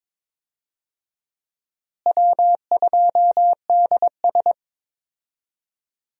Morse code, heard as a keyed tone, about 22 wpm, 700 Hz. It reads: W2DS
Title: USA Call Signs - 4 Characters - Repeated 3x - 22wpm